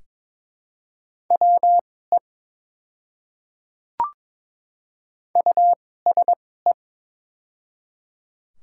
Morse code, heard as a keyed tone, about 21 wpm, spaced as for 16 wpm, 690 Hz. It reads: WE USE
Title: 2x Character Spacing - Top 100 Words - 22wpm